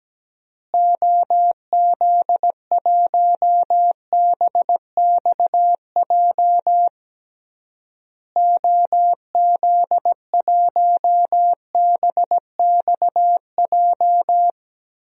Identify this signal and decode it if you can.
Morse code, heard as a keyed tone, about 17 wpm, 700 Hz. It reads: OZ1BXJ OZ1BXJ
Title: DXCC-100 Call Signs - 6 Characters - Repeated 3x - 17wpm